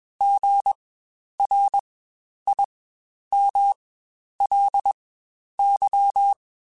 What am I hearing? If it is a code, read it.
Morse code, heard as a keyed tone, about 21 wpm, spaced as for 10 wpm, 800 Hz. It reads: GRIMLY
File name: Spaced Alphas